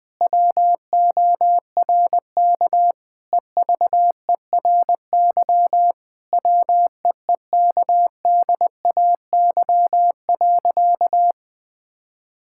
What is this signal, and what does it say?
Morse code, heard as a keyed tone, about 20 wpm, 700 Hz. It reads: WORK EVERY WEEKDAY.
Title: Difficult Sentences with Hints - 20wpm